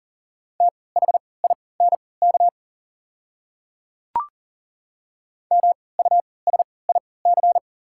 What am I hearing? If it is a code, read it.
Morse code, heard as a keyed tone, about 40 wpm, spaced as for 23 wpm, 690 Hz. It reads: THINK MUSIC
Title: 3x Character Spacing - Top 300 Words - 40wpm